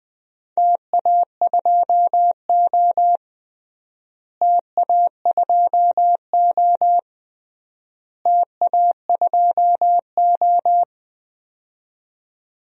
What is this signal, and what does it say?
Morse code, heard as a keyed tone, about 21 wpm, 700 Hz. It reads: TA2O TA2O TA2O
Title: DXCC-100 Call Signs - 4 Characters - Repeated 3x - 20wpm